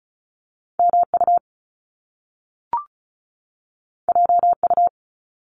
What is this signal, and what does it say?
Morse code, heard as a keyed tone, about 36 wpm, 700 Hz. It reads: MU JU
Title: Top Two Letter Combinations in Words 35wpm